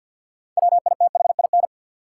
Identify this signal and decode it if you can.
Morse code, heard as a keyed tone, about 53 wpm, 700 Hz. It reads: WITHIN
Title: Sets of 2 Words - New Word Drill - 50wpm